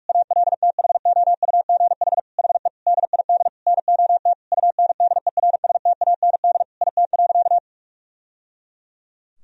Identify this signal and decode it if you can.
Morse code, heard as a keyed tone, about 46 wpm, 700 Hz. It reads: ALTHOUGH HE DID NOT UNDERSTAND IT.
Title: Sentences from Top 1000 Words - Review Part 3 - 45wpm